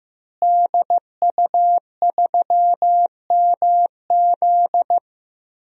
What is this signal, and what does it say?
Morse code, heard as a keyed tone, about 15 wpm, 700 Hz. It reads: DU3MZ
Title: DXCC-100 Call Signs - 5 Characters - Repeated 2x - 15wpm